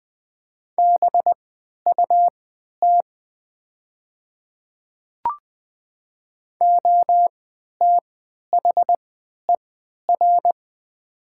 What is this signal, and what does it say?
Morse code, heard as a keyed tone, about 20 wpm, spaced as for 11 wpm, 700 Hz. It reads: BUT OTHER
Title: 3x Character Spacing - Top 100 Words - 20wpm